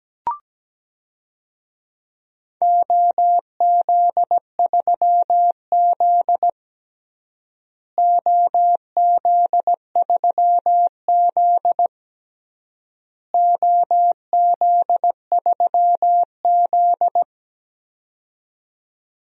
Morse code, heard as a keyed tone, about 17 wpm, 700 Hz. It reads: OZ3Z OZ3Z OZ3Z
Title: DXCC-100 Call Signs - 4 Characters - Repeated 3x - 17wpm